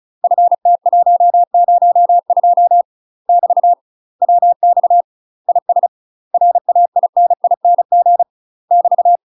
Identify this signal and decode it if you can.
Morse code, heard as a keyed tone, about 36 wpm, 700 Hz. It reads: FT102 <BT> WX IS RAINING <BT>
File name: UK MASTER QSOs TRAINING / 35WPM / UK MASTER QSOs 0041